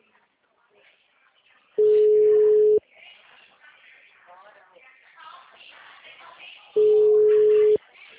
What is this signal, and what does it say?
Ringback tone (Europe, 425 Hz), with speech.